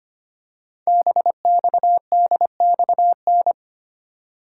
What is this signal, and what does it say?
Morse code, heard as a keyed tone, about 25 wpm, 700 Hz. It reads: BXDXN
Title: Code Groups - 5 Letter:Numbers - 25wpm